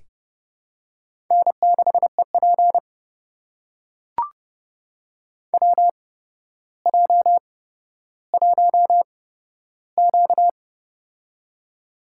Morse code, heard as a keyed tone, about 29 wpm, spaced as for 13 wpm, 700 Hz. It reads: N6EP WJ1Q
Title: USA Call Signs - 8x Character Spacing - 30wpm